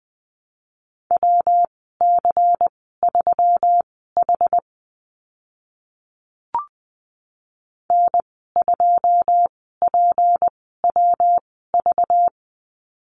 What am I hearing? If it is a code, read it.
Morse code, heard as a keyed tone, about 20 wpm, spaced as for 15 wpm, 700 Hz. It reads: WC3H N2PWV
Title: 193 Lesson 31g - Letter v - Call Signs 20wpm